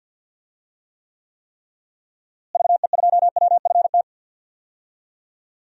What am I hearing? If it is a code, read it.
Morse code, heard as a keyed tone, about 52 wpm, 700 Hz. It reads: VE2WFT
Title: DXCC-100 Call Signs - 6 Characters - Repeated 3x - 50wpm